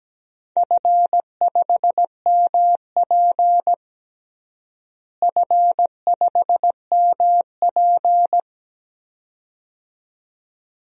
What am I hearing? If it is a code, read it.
Morse code, heard as a keyed tone, about 17 wpm, 700 Hz. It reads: F5MP F5MP
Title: DXCC-100 Call Signs - 4 Characters - Repeated 2x - 17wpm